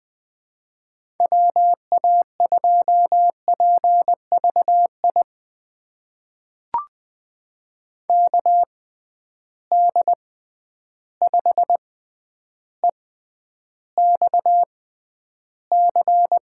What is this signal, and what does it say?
Morse code, heard as a keyed tone, about 20 wpm, spaced as for 10 wpm, 700 Hz. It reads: WA2PVI KD5EXC
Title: USA Call Signs - 6 Characters - 6x Character Spacing - 20wpm